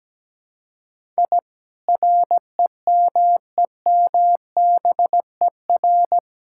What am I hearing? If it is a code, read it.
Morse code, heard as a keyed tone, about 17 wpm, 700 Hz. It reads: I REMEMBER
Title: Sentences from Top 1000 Words - Review Part 5 - 17wpm